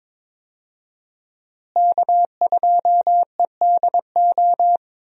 Morse code, heard as a keyed tone, about 22 wpm, 700 Hz. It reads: K2EDO
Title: USA Call Signs - 5 Characters - Repeated 3x - 22wpm